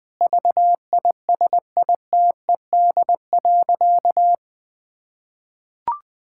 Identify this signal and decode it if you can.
Morse code, heard as a keyed tone, about 20 wpm, 700 Hz. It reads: VISITED.
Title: Sentences with 2-Word Phrases - 20wpm